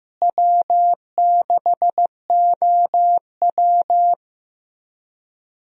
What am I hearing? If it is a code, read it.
Morse code, heard as a keyed tone, about 15 wpm, 700 Hz. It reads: W6OW
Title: USA Call Signs - 4 Characters - Repeated 3x - 15wpm